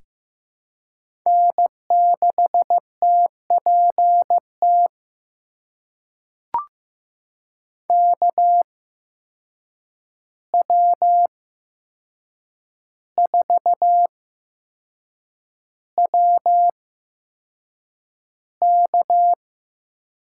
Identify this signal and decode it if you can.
Morse code, heard as a keyed tone, about 15 wpm, spaced as for 6 wpm, 700 Hz. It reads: N6TPT KW4WK
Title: USA Call Signs - 5 Characters - 8x Character Spacing - 15wpm